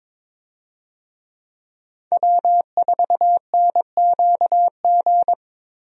Morse code, heard as a keyed tone, about 22 wpm, 700 Hz. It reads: W4NQG
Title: USA Call Signs - 5 Characters - Repeated 2x - 22wpm